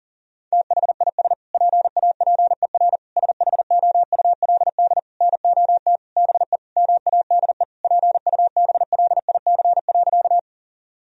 Morse code, heard as a keyed tone, about 37 wpm, 700 Hz. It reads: THIS PAPER SHOULD NOT BE MADE PUBLIC.
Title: Sentences from Top 1000 Words - Review Part 3 - 40wpm